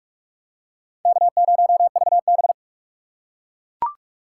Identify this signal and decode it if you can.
Morse code, heard as a keyed tone, about 42 wpm, 700 Hz. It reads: K0VB
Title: USA Call Signs - 4 Characters - 6x Character Spacing - 45wpm